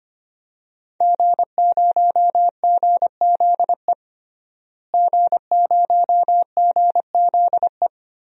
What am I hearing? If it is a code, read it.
Morse code, heard as a keyed tone, about 25 wpm, 700 Hz. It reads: G0GZE G0GZE
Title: DXCC-100 Call Signs - 5 Characters - Repeated 3x - 25wpm